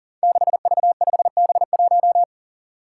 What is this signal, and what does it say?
Morse code, heard as a keyed tone, about 42 wpm, 700 Hz. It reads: 6V5B1
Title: Five Challenging Letter-Number Combinations 40wpm